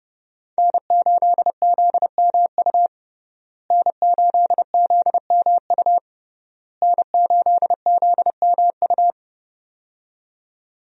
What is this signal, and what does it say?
Morse code, heard as a keyed tone, about 32 wpm, 700 Hz. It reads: N8ZMU N8ZMU N8ZMU
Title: USA Call Signs - 5 Characters - Repeated 3x - 30wpm